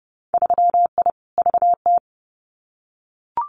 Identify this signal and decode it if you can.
Morse code, heard as a keyed tone, about 30 wpm, 700 Hz. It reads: 3I VT
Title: Winter Field Day Exchanges Encore 30wpm